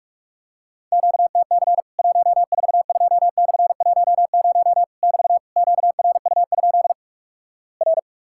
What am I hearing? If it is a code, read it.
Morse code, heard as a keyed tone, about 44 wpm, 700 Hz. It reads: QTC 142/10 <BT> QRU? R
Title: Worked All Europe DX Contest - QTC Exchange - 45wpm